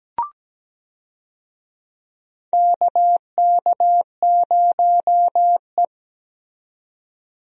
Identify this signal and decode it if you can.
Morse code, heard as a keyed tone, about 17 wpm, 700 Hz. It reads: KK0E
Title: USA Call Signs - 4 Characters - 17wpm